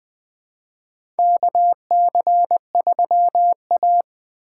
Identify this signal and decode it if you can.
Morse code, heard as a keyed tone, about 20 wpm, 700 Hz. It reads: K C 3 A
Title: USA Call Signs - 4 Characters - 6x Character Spacing - 20wpm